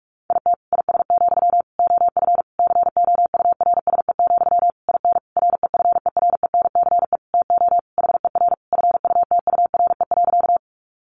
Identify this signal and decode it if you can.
Morse code, heard as a keyed tone, about 44 wpm, 700 Hz. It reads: IT IS, OF COURSE, IN REFERENCE TO HER FUTURE.
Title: Sentences from Top 1200 Words 45wpm